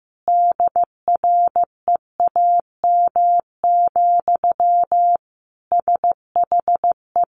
Morse code, heard as a keyed tone, about 15 wpm, 700 Hz. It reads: DREAM, SHE
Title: Sentences from Top 1500 Words 15wpm